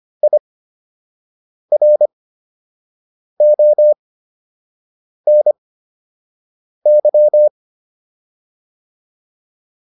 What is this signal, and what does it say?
Morse code, heard as a keyed tone, about 25 wpm, spaced as for 6 wpm, 600 Hz. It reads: IRONY